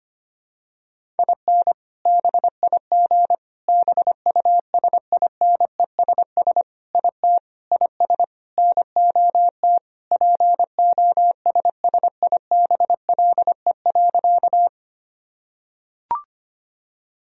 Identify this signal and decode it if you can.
Morse code, heard as a keyed tone, about 25 wpm, 700 Hz. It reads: IN BIG BUSINESS IT IS NOT POSSIBLE.
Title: Sentences from Top 1000 Words - Review Part 1 - 25wpm